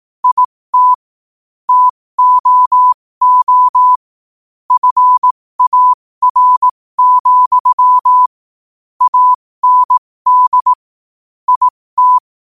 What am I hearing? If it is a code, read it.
Morse code, heard as a keyed tone, about 18 wpm, spaced as for 15 wpm, 1000 Hz. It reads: IT TOO FAR, AND IT
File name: qotdfiles / 15wpm